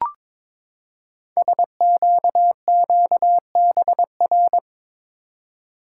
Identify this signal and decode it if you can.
Morse code, heard as a keyed tone, about 22 wpm, 700 Hz. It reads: SQQBR
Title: Code Groups - 5 Letters - 22wpm